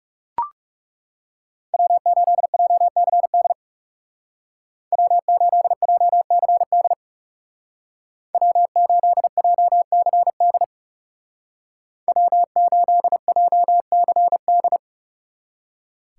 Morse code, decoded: W8JCD W8JCD W8JCD W8JCD

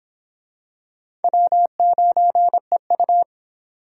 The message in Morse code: W9EU

